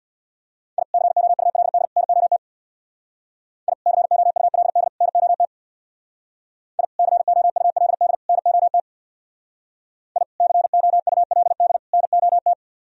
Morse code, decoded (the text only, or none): I COULD NOT I COULD NOT I COULD NOT I COULD NOT